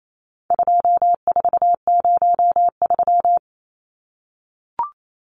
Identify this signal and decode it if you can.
Morse code, heard as a keyed tone, about 28 wpm, 700 Hz. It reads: 2403
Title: Numbers - 4-Digit 28wpm